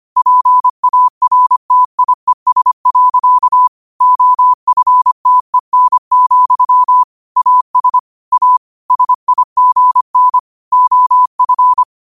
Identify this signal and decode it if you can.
Morse code, heard as a keyed tone, about 25 wpm, 1000 Hz. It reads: PARTIES. OFTEN, AS A SIGN OF
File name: qotdfiles / 25wpm